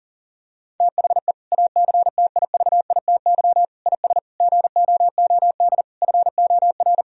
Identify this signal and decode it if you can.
Morse code, heard as a keyed tone, about 40 wpm, 700 Hz. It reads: THE ACTIVITY IS GOOD FOR
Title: Sentences from Top 1000 Words - Review Part 4 - 40wpm